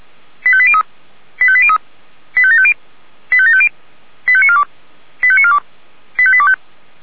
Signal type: five-tone selective call